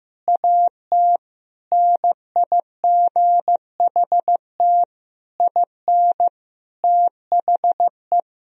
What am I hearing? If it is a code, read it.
Morse code, heard as a keyed tone, about 15 wpm, 700 Hz. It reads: AT NIGHT IN THE
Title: Sentences from Top 1000 Words - Review Part 1 - 15wpm